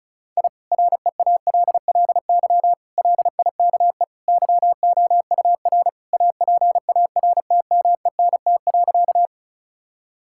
Morse code, heard as a keyed tone, about 36 wpm, 700 Hz. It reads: I REALLY LIKE YOUR APARTMENT.
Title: Sentences from Top 1000 Words - Review Part 6 - 35wpm